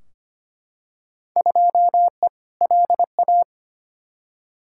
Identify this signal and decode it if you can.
Morse code, heard as a keyed tone, about 25 wpm, 700 Hz. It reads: 2E LA
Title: Field Day Exchanges Encore - 25wpm